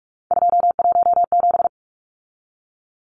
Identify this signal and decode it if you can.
Morse code, heard as a keyed tone, about 44 wpm, 700 Hz. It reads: 217
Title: Numbers - 3-Digit 45wpm